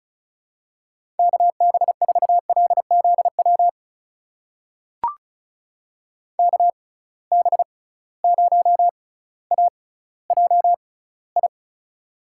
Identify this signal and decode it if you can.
Morse code, heard as a keyed tone, about 36 wpm, spaced as for 18 wpm, 700 Hz. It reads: KB4LZW KB0AJI
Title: USA Call Signs - 6 Characters - 6x Character Spacing - 35wpm